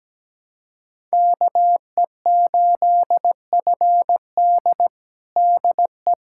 Morse code, heard as a keyed tone, about 17 wpm, 700 Hz. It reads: KE8FD DE
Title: USA Callsign DE Callsign - 5 Characters - 17wpm